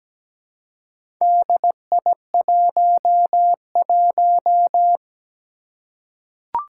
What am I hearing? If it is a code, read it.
Morse code, heard as a keyed tone, about 17 wpm, 700 Hz. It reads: DI11
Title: Four Challenging Letter-Number Combinations 17wpm